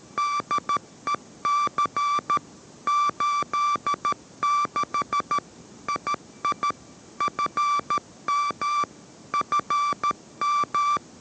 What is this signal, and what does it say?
Morse code, heard as a keyed tone, about 13 wpm, 1170 Hz. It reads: DEC 86 II FM FM